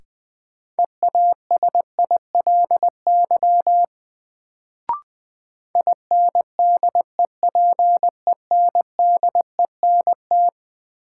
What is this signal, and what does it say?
Morse code, heard as a keyed tone, about 20 wpm, 700 Hz. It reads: EASILY INDEPENDENT
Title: Sentences from Top 900 Words - New Word Drill - 20wpm